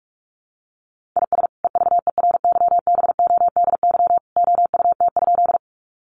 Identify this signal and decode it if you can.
Morse code, heard as a keyed tone, about 44 wpm, 700 Hz. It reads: IS EVERYBODY OUT?
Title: Sentences from Top 1400 Words 45wpm